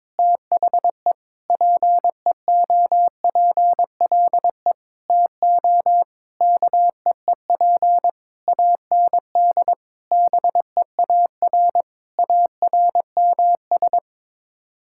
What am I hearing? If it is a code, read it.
Morse code, heard as a keyed tone, about 21 wpm, 700 Hz. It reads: THE PEOPLE TO KEEP AND BEAR ARMS
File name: Sets of 9 Words - 22wpm